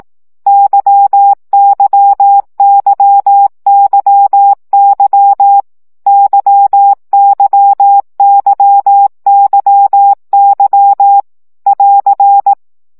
Morse code, heard as a keyed tone, about 18 wpm, 800 Hz. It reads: YYYYY YYYYY <AR>
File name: CW gyakorlas y 15 baud